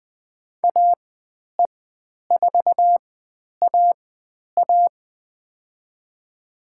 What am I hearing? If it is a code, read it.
Morse code, heard as a keyed tone, about 20 wpm, spaced as for 10 wpm, 700 Hz. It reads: AE4A A
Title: 109 Lesson 18g - Letter m - Call Signs 20wpm